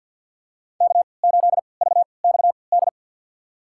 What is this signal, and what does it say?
Morse code, heard as a keyed tone, about 52 wpm, spaced as for 29 wpm, 700 Hz. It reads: K8VXD
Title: USA Call Signs - 5 Characters - 3x Character Spacing - 50wpm